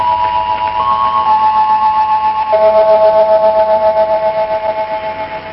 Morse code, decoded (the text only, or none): none